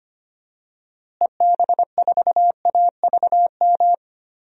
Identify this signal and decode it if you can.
Morse code, heard as a keyed tone, about 25 wpm, 700 Hz. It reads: EB4AVM
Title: DXCC-100 Call Signs - 6 Characters - Repeated 2x - 25wpm